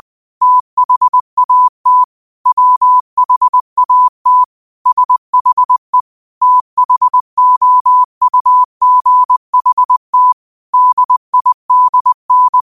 Morse code, heard as a keyed tone, about 20 wpm, 1000 Hz. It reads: THAT WHAT SHE THOUGHT DIDN
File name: qotdfiles / 20wpm